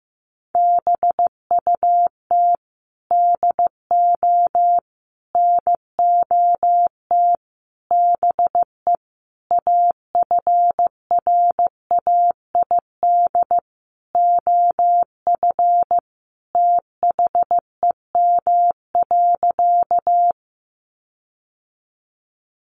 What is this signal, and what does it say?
Morse code, heard as a keyed tone, about 15 wpm, 700 Hz. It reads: BUT DO NOT BE AFRAID OF THEM.
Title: Sentences from Top 1500 Words 15wpm